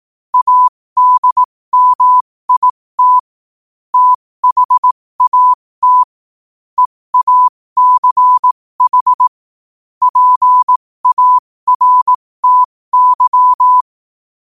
Morse code, heard as a keyed tone, about 18 wpm, spaced as for 15 wpm, 1000 Hz. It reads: ADMIT THAT EACH PARTY
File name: qotdfiles / 15wpm